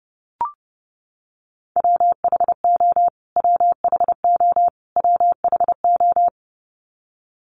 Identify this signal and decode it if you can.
Morse code, heard as a keyed tone, about 30 wpm, 700 Hz. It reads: WHO WHO WHO
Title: Top 200 Words Repeated 30wpm